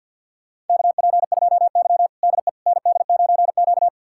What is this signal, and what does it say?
Morse code, heard as a keyed tone, about 53 wpm, 700 Hz. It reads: KP2Y DE ND9Q
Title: USA Callsign DE Callsign - 4 Characters - 50wpm